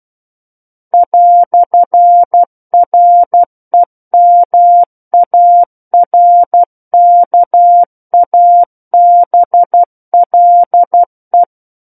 Morse code, heard as a keyed tone, about 12 wpm, 700 Hz. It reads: "REMARKABLE
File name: Book One / 03-Chapter 3 - 12 WPM